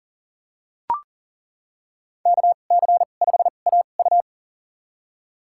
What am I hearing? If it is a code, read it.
Morse code, heard as a keyed tone, about 40 wpm, spaced as for 29 wpm, 700 Hz. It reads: KC5AU
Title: USA Call Signs - 5 Characters - 2x Character Spacing - 40wpm